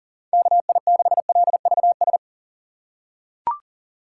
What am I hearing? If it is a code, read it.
Morse code, heard as a keyed tone, about 40 wpm, 700 Hz. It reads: KI6LVS
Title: USA Call Signs - 6 Characters - 4x Character Spacing - 40wpm